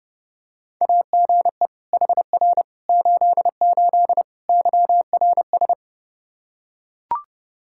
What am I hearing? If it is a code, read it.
Morse code, heard as a keyed tone, about 32 wpm, 700 Hz. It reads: AGE HR 88 YRS
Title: Practice QSO - Exchange 3 - More Info - 30wpm